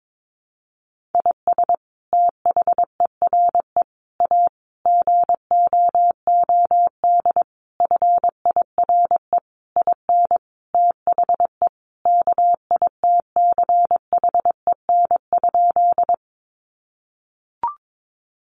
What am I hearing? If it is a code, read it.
Morse code, heard as a keyed tone, about 22 wpm, 700 Hz. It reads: IS THERE A GOOD FIRE IN THE KITCHEN?
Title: Sentences from Top 1400 Words 22wpm